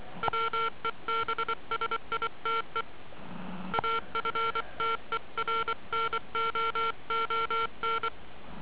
Morse code, heard as a keyed tone, about 24 wpm, 1360 Hz. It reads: WEBSITE AFTERNOON